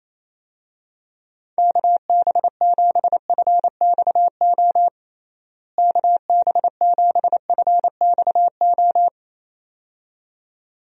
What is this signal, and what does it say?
Morse code, heard as a keyed tone, about 28 wpm, 700 Hz. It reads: KB7FXO KB7FXO
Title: USA Call Signs - 6 Characters - Repeated 2x - 28wpm